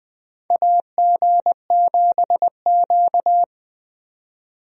Morse code, heard as a keyed tone, about 20 wpm, 700 Hz. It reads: AG7Q
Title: USA Call Signs - 4 Characters - 20wpm